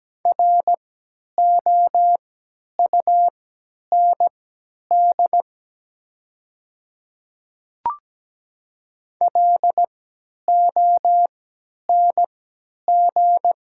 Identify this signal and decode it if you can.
Morse code, heard as a keyed tone, about 17 wpm, spaced as for 10 wpm, 700 Hz. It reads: ROUND LONG
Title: 3x Character Spacing - Top 500 Words - 17wpm